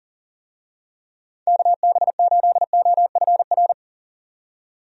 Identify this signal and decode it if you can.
Morse code, heard as a keyed tone, about 38 wpm, 700 Hz. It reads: KB8OFR
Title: USA Call Signs - 6 Characters - 40wpm